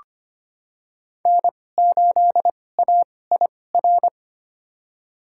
Morse code, heard as a keyed tone, about 25 wpm, spaced as for 18 wpm, 700 Hz. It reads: N8AIR
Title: USA Call Signs - 5 Characters - 2x Character Spacing - 25wpm